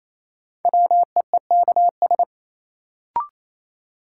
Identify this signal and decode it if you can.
Morse code, heard as a keyed tone, about 28 wpm, 700 Hz. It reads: WEEKS
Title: Sets of 5 Words - New Word Drill - 28wpm